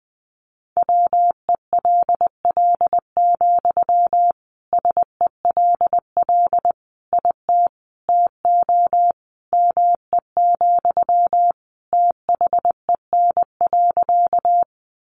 Morse code, heard as a keyed tone, about 20 wpm, 700 Hz. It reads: WELL, SELL IT TO ME, THEN.
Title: Sentences from Top 1100 Words 20wpm